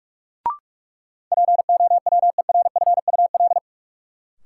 Morse code, heard as a keyed tone, about 46 wpm, 700 Hz. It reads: POWERFUL
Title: Sentences from Top 1300 Words - New Word Drill Alt - 45wpm